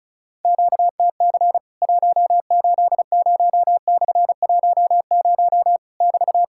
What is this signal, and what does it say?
Morse code, heard as a keyed tone, about 36 wpm, 700 Hz. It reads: QTC 180/10 <BT>
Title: Worked All Europe DX Contest - QTC Exchange - 35wpm